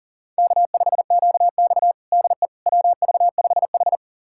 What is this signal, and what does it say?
Morse code, heard as a keyed tone, about 40 wpm, 700 Hz. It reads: K5QX DE WV5H
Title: USA Callsign DE Callsign - 4 Characters - 40wpm